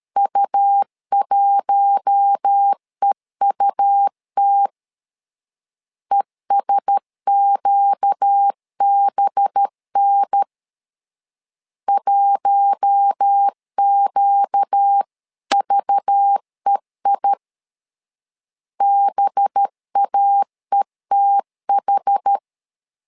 Morse code, heard as a keyed tone, about 13 wpm, 800 Hz. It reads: U1EUT ESQBN 1QVEI BAETH
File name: LEZIONI / 5 LBVC / 5 LBV 12wpm 1855